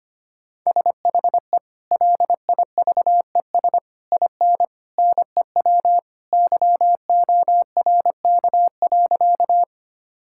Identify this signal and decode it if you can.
Morse code, heard as a keyed tone, about 25 wpm, 700 Hz. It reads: SHE LIVES IN NEW YORK.
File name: practice / 25wpm / Sentences with 3-Word Phrases - 25wpm